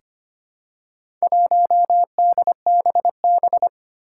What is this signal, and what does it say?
Morse code, heard as a keyed tone, about 25 wpm, 700 Hz. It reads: 1DBB
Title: Four Challenging Letter-Number Combinations 25wpm